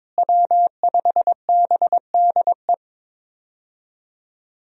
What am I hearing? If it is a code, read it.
Morse code, heard as a keyed tone, about 23 wpm, 700 Hz. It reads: W5BDE
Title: USA Call Signs - 5 Characters - Repeated 2x - 22wpm